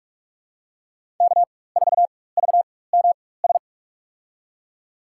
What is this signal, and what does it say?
Morse code, heard as a keyed tone, about 47 wpm, spaced as for 21 wpm, 700 Hz. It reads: K4VMS
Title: USA Call Signs - 4x Character Spacing - 45wpm